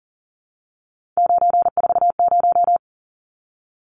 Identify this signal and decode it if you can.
Morse code, heard as a keyed tone, about 40 wpm, 700 Hz. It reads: 940
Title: Numbers - 3-Digit 40wpm